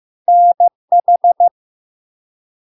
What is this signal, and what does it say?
Morse code, heard as a keyed tone, about 15 wpm, 700 Hz. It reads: NH
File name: LARK-FD-2022-Log-in-CW-15wpm-700-Hz